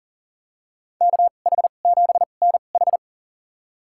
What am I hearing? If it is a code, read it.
Morse code, heard as a keyed tone, about 40 wpm, spaced as for 29 wpm, 700 Hz. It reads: KH7NH
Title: USA Call Signs - 5 Characters - 2x Character Spacing - 40wpm